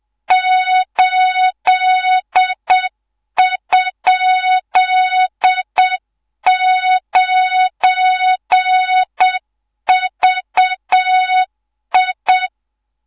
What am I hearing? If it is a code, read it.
Morse code, heard as a keyed tone, about 7 wpm, 3700 Hz, with noise. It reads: 8?9VI
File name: Morze-HA2MN / hangp / 2 03 GYAKORLAT